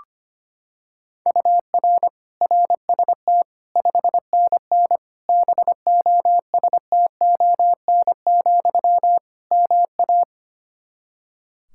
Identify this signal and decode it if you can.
Morse code, heard as a keyed tone, about 25 wpm, 700 Hz. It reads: UR RST 5NN BOSTON, MA
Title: Practice QSO - Exchange 1 - Beginning - 25wpm